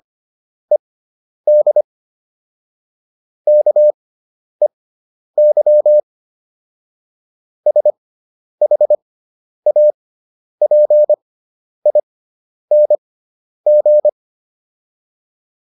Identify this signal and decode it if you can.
Morse code, heard as a keyed tone, about 25 wpm, spaced as for 10 wpm, 600 Hz. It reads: ED KEY SHAPING